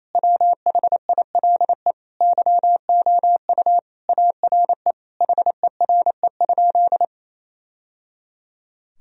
Morse code, heard as a keyed tone, about 28 wpm, 700 Hz. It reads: WHILE YOU ARE HERE?
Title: Sentences from Top 1000 Words - Review Part 1 - 28wpm